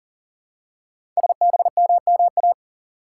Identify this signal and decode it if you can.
Morse code, heard as a keyed tone, about 41 wpm, 700 Hz. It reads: SBMMA
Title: Code Groups - 5 Letters - 40wpm